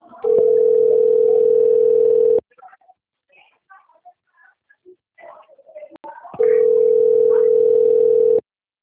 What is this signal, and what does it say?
Ringback tone (North America), 440 + 480 Hz.